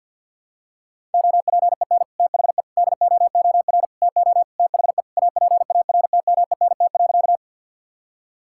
Morse code, heard as a keyed tone, about 52 wpm, 700 Hz. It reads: OPEN THE DOOR TO THE APARTMENT.
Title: Sentences from Top 1000 Words - Review Part 6 - 50wpm